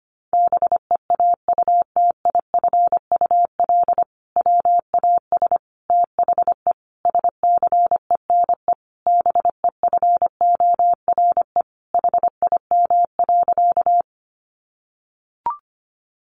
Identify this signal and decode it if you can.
Morse code, heard as a keyed tone, about 25 wpm, 700 Hz. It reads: BEAUTIFUL WAS THE SCENE BEFORE HIM.
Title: Sentences from Top 1100 Words 25wpm